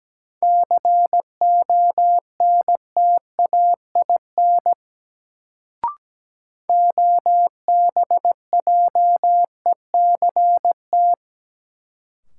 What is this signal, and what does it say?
Morse code, heard as a keyed tone, about 17 wpm, 700 Hz. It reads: CONTAIN OBJECT
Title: Sentences from Top 500 Words - New Word Drill - 17wpm